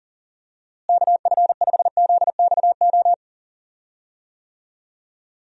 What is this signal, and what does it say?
Morse code, heard as a keyed tone, about 40 wpm, 700 Hz. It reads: KF5ZXO
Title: USA Call Signs - 6 Characters - Repeated 3x - 40wpm